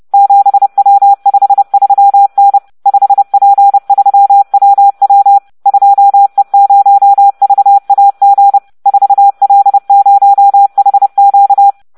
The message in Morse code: 7W53N 5P3WW 2E0VAG 4L0HQ